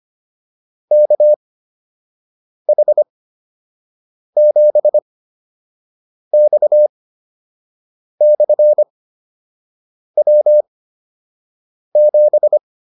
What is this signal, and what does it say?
Morse code, heard as a keyed tone, about 25 wpm, spaced as for 6 wpm, 600 Hz. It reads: KH7X/W7